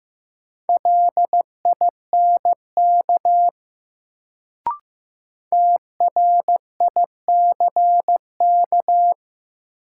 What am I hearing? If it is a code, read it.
Morse code, heard as a keyed tone, about 15 wpm, 700 Hz. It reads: LINK TRICK
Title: Sentences from Top 1500 Words - New Word Drill Alt - 15wpm